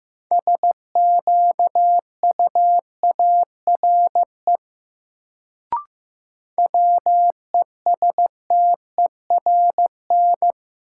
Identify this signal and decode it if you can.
Morse code, heard as a keyed tone, about 15 wpm, 700 Hz. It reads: SQUARE WESTERN
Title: Sentences from Top 1000 Words - Review Part 7 - New Word Drill - 15wpm